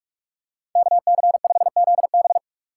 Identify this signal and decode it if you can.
Morse code, heard as a keyed tone, about 46 wpm, 690 Hz. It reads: KC5ZB
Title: USA Call Signs - 4x Character Spacing - 45wpm